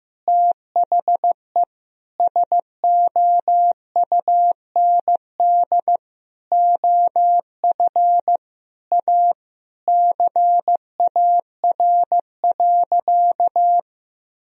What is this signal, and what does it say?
Morse code, heard as a keyed tone, about 16 wpm, 700 Hz. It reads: THE SOUND OF A CAR.
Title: Sentences from Top 1000 Words - Review Part 1 - 15wpm